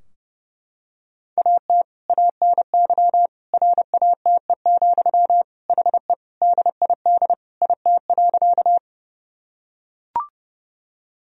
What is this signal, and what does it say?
Morse code, heard as a keyed tone, about 30 wpm, 700 Hz. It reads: AT ANY RATE, HE DID IT.
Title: Sentences from Top 1000 Words - Review Part 4 - 30wpm